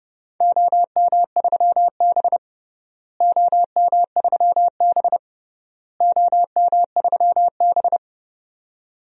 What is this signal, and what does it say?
Morse code, heard as a keyed tone, about 29 wpm, 700 Hz. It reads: OM3B OM3B OM3B
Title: DXCC-100 Call Signs - 4 Characters - Repeated 3x - 30wpm